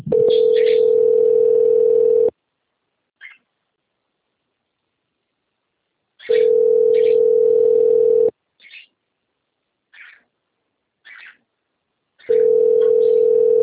Ringback tone (North America), 440 + 480 Hz.
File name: rg-1600-unknown- - - .3596